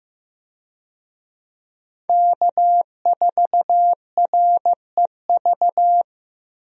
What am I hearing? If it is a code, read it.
Morse code, heard as a keyed tone, about 15 wpm, 700 Hz. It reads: K4REV
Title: USA Call Signs - 5 Characters - Repeated 2x - 15wpm